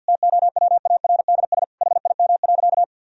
Morse code, heard as a keyed tone, about 50 wpm, 700 Hz. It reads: TOWARDS HIM.
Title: Sentences from Top 1000 Words - Review Part 2 - 50wpm